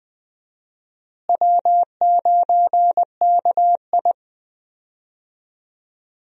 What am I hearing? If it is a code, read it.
Morse code, heard as a keyed tone, about 21 wpm, 700 Hz. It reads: W9KI
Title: USA Call Signs - 4 Characters - Repeated 3x - 20wpm